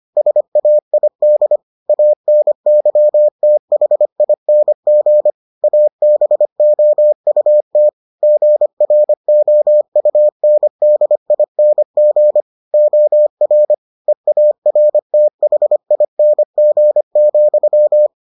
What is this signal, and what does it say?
Morse code, heard as a keyed tone, about 25 wpm, 600 Hz. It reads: SAID ANYTHING ABOUT GROUNDING OR EARTHING,